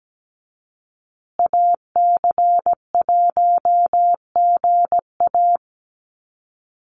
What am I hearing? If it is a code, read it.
Morse code, heard as a keyed tone, about 17 wpm, 700 Hz. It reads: AC1GA
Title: USA Call Signs Repeated 2x 17wpm